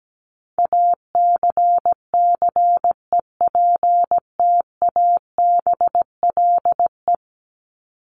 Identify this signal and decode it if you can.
Morse code, heard as a keyed tone, about 17 wpm, 700 Hz. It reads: ACCEPTABLE
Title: Top 1501-1750 Words 17wpm